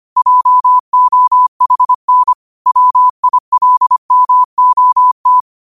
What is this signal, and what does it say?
Morse code, heard as a keyed tone, about 25 wpm, 1000 Hz. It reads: JOHN WILMOT
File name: qotdfiles / 25wpm